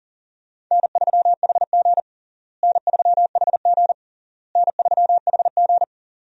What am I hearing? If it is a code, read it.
Morse code, heard as a keyed tone, about 40 wpm, 700 Hz. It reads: N3HG N3HG N3HG
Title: USA Call Signs - 4 Characters - Repeated 3x - 40wpm